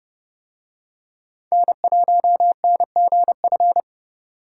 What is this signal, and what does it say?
Morse code, heard as a keyed tone, about 30 wpm, 700 Hz. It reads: N1NGF